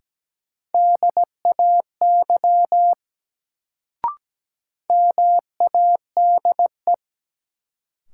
Morse code, heard as a keyed tone, about 17 wpm, 700 Hz. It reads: DAY MADE
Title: Sentences from Top 200 Words - New Word Drill - 17wpm